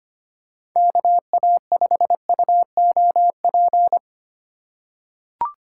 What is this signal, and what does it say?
Morse code, heard as a keyed tone, about 25 wpm, 700 Hz. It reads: KA5UOP
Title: USA Call Signs - 6 Characters - 2x Character Spacing - 25wpm